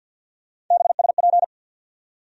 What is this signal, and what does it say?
Morse code, heard as a keyed tone, about 54 wpm, 700 Hz. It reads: BSP
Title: USA Call Sign Postfixes - 3 Letters 50wpm